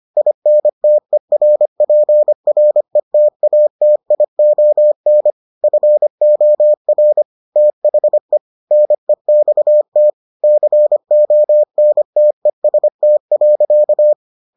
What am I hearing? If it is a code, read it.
Morse code, heard as a keyed tone, about 25 wpm, 600 Hz. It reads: INTERPRETATION FOR THE NEXT CONTEST.